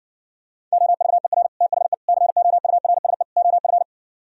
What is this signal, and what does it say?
Morse code, heard as a keyed tone, about 69 wpm, 700 Hz. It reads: OVER THE COURSE OF